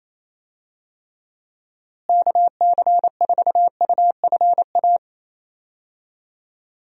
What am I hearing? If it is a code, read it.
Morse code, heard as a keyed tone, about 28 wpm, 700 Hz. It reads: KC4UFA
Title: USA Call Signs - 6 Characters - Repeated 2x - 28wpm